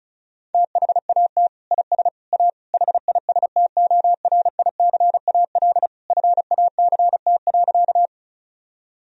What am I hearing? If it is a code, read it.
Morse code, heard as a keyed tone, about 36 wpm, 700 Hz. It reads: THAT IS A HISTORICAL FACT.
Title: Sentences from Top 1000 Words - Review Part 4 - 35wpm